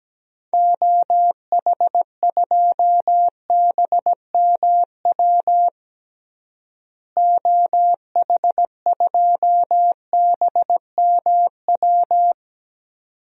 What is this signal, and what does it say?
Morse code, heard as a keyed tone, about 17 wpm, 700 Hz. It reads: OH2BMW OH2BMW
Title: DXCC-100 Call Signs - 6 Characters - Repeated 2x - 17wpm